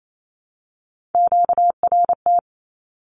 Morse code, heard as a keyed tone, about 28 wpm, 700 Hz. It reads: QRT
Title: Q-Codes Repeated 3x 28wpm